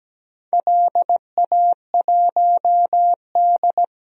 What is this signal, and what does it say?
Morse code, heard as a keyed tone, about 17 wpm, 700 Hz. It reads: LA1D
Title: DXCC-100 Call Signs - 4 Characters - Repeated 3x - 17wpm